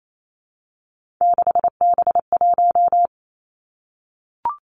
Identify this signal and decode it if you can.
Morse code, heard as a keyed tone, about 28 wpm, 700 Hz. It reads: 6B1
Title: Three Challenging Letter-Number Combinations 28wpm